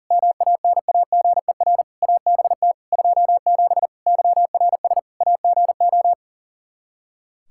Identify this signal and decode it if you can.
Morse code, heard as a keyed tone, about 40 wpm, 700 Hz. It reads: MANAGER ABT 27 YRS AGO
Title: Practice QSO - Exchange 3 - More Info - 40wpm